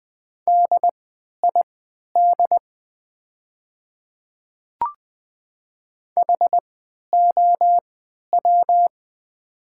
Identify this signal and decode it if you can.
Morse code, heard as a keyed tone, about 20 wpm, spaced as for 11 wpm, 700 Hz. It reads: DID HOW
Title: 3x Character Spacing - Top 100 Words - 20wpm